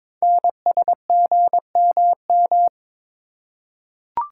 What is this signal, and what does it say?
Morse code, heard as a keyed tone, about 22 wpm, 700 Hz. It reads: NSGMM
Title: Code Groups - 5 Letters - 22wpm